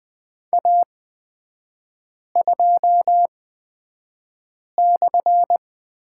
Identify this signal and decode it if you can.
Morse code, heard as a keyed tone, about 20 wpm, spaced as for 5 wpm, 700 Hz. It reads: A2/
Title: 224 Lesson 36e - BT - Three Characters 20wpm